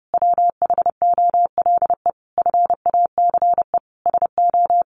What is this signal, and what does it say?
Morse code, heard as a keyed tone, about 30 wpm, 700 Hz. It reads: WHOLE FACE SO
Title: Sentences from Top 1400 Words 30wpm